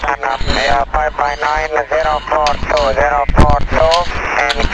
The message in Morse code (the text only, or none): none